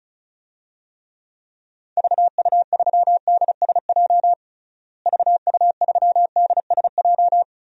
Morse code, heard as a keyed tone, about 36 wpm, 700 Hz. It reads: VU3DSJ VU3DSJ
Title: DXCC-100 Call Signs - 6 Characters - Repeated 3x - 35wpm